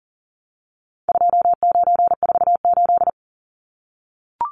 Morse code, heard as a keyed tone, about 40 wpm, 700 Hz. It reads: 2948
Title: Numbers - 4-Digit 40wpm